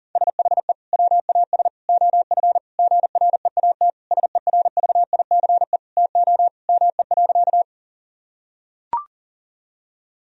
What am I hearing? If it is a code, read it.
Morse code, heard as a keyed tone, about 37 wpm, 700 Hz. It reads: SHE WAS OF GREAT SERVICE TO ME.
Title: Sentences from Top 1000 Words - Review Part 4 - 40wpm